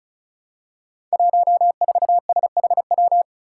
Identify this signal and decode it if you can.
Morse code, heard as a keyed tone, about 34 wpm, 700 Hz. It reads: 14SHW
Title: Five Challenging Letter-Number Combinations 35wpm